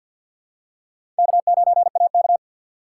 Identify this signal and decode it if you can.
Morse code, heard as a keyed tone, about 49 wpm, 700 Hz. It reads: K9AK
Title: USA Call Signs - 4 Characters - 3x Character Spacing - 50wpm